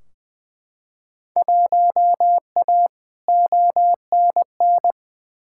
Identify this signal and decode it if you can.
Morse code, heard as a keyed tone, about 20 wpm, 700 Hz. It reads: 1A ONN
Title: Field Day Exchanges Encore - 20wpm